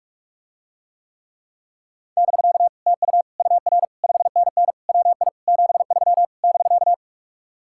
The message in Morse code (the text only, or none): <BK> TU UR 5NN WI 73 <BK>